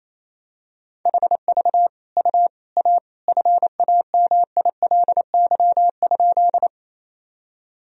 Morse code, heard as a keyed tone, about 28 wpm, 700 Hz. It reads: HV U A FAMILY?
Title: Sentences from Top 400 Words and Abbreviations 28wpm